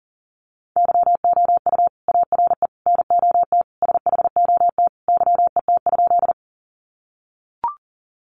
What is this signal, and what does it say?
Morse code, heard as a keyed tone, about 40 wpm, 700 Hz. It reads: YOU ARE NOT SHOT YET?
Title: Sentences from Top 1100 Words 40wpm